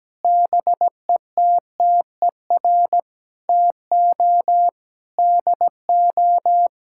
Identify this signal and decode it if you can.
Morse code, heard as a keyed tone, about 17 wpm, 700 Hz. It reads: BETTER TO DO
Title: Sentences from Top 1600 Words 17wpm